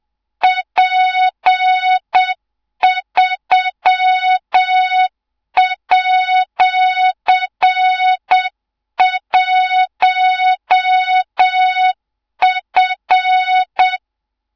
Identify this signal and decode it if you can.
Morse code, heard as a keyed tone, about 7 wpm, 3700 Hz, with noise. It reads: P3@1F
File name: Morze-HA2MN / hang / 24 GYAKORLAT